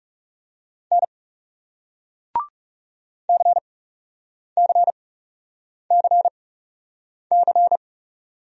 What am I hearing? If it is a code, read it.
Morse code, heard as a keyed tone, about 38 wpm, spaced as for 8 wpm, 700 Hz. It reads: N CCCC